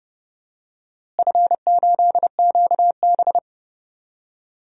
Morse code, heard as a keyed tone, about 30 wpm, 700 Hz. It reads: F8QB
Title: DXCC-100 Call Signs - 4 Characters - Repeated 3x - 30wpm